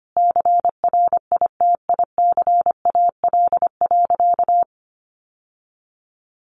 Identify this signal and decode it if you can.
Morse code, heard as a keyed tone, about 25 wpm, 700 Hz. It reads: CRITICAL.
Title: Sentences from Top 1100 Words 25wpm